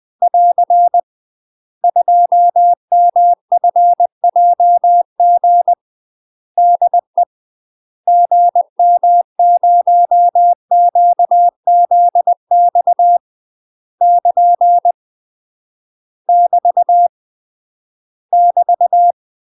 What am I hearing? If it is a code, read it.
Morse code, heard as a keyed tone, about 20 wpm, 700 Hz. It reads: <AR> 2MFJG DE GM0QZX <KN> <BT> <BT>
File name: UK MASTER QSOs TRAINING/20WPM